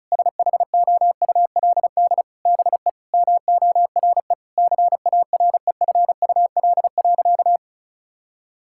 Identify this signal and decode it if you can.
Morse code, heard as a keyed tone, about 36 wpm, 700 Hz. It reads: SHOULD BE MORE CAREFUL.
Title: Sentences from Top 1600 Words 35wpm